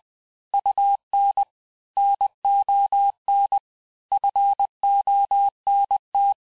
Morse code, heard as a keyed tone, about 20 wpm, 800 Hz. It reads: UN NON FONT
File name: cours-du-mercredi / audios / cours-858-texte-clais-vitesse-19mpm